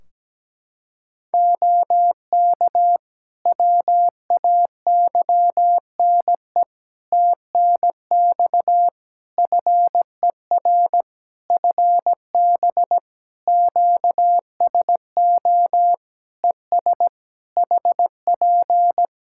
Morse code, heard as a keyed tone, about 17 wpm, 700 Hz. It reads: OK WAYNE TNX FER FB QSO ES HP
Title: Practice QSO - Exchange 4 - Ending - 17wpm